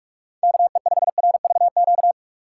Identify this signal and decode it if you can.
Morse code, heard as a keyed tone, about 46 wpm, 700 Hz. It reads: KE5RVQ
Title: USA Call Signs - 6 Characters - 2x Character Spacing - 45wpm